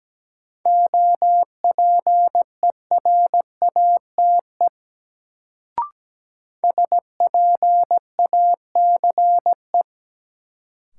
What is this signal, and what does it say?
Morse code, heard as a keyed tone, about 17 wpm, 700 Hz. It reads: OPERATE SPACE
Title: Sentences from Top 1500 Words - New Word Drill Alt - 17wpm